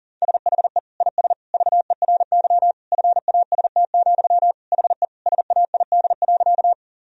Morse code, heard as a keyed tone, about 37 wpm, 700 Hz. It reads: SHE IS VERY FAST, HE SAID.
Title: Sentences from Top 1000 Words - Review Part 2 - 40wpm